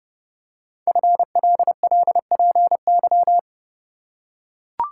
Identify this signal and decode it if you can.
Morse code, heard as a keyed tone, about 30 wpm, 700 Hz. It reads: FLLPY
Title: Code Groups - 5 Letters - 30wpm